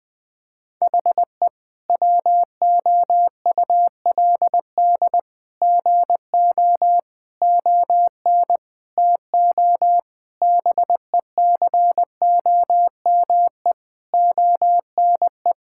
Morse code, heard as a keyed tone, about 21 wpm, 700 Hz. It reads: HE WOULD GO ON TO BECOME ONE
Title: Sentences with 8-Word Phrases - 20wpm